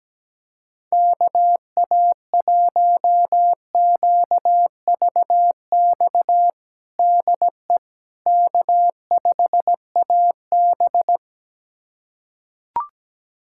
Morse code, heard as a keyed tone, about 17 wpm, 700 Hz. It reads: KA1QVX DE K5AB
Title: USA Callsign DE Callsign 17wpm